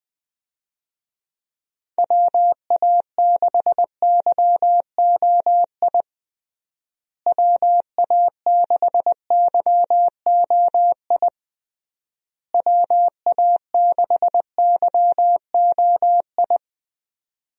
Morse code, heard as a keyed tone, about 20 wpm, 700 Hz. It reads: WA6YOI WA6YOI WA6YOI
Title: USA Call Signs - 6 Characters - Repeated 3x - 20wpm